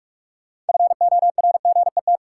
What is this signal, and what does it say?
Morse code, heard as a keyed tone, about 43 wpm, 700 Hz. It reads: FORGET
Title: Sentences from Top 1200 Words - New Word Drill Alt - 45wpm